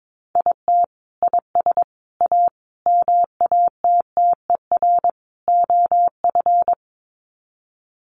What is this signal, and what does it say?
Morse code, heard as a keyed tone, about 22 wpm, 700 Hz. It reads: IT IS A MATTER OF
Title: Sets of 5 Words Encore2 22wpm